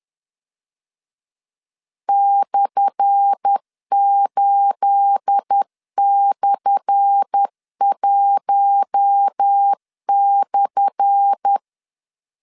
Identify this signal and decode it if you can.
Morse code, heard as a keyed tone, about 10 wpm, 800 Hz. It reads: /8/1/
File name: LEZIONI / 10 SIGNS / 10A SLASH 10wpm 2808